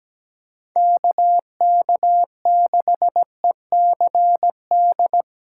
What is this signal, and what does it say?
Morse code, heard as a keyed tone, about 17 wpm, 700 Hz. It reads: KK6ECD
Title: USA Callsign DE Callsign - 6 Characters - 17wpm